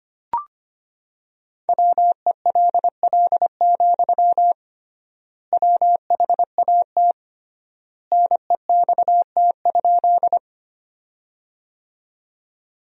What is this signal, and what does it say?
Morse code, heard as a keyed tone, about 25 wpm, 700 Hz. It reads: WELL, WHAT NEXT?
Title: 3x Word-Spacing - Sentences from Top 300 Words - 25wpm